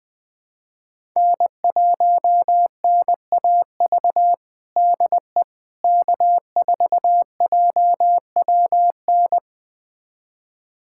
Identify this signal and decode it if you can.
Morse code, heard as a keyed tone, about 20 wpm, 700 Hz. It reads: N1NAV DE K4JWN
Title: USA Callsign DE Callsign - 5 Characters - 20wpm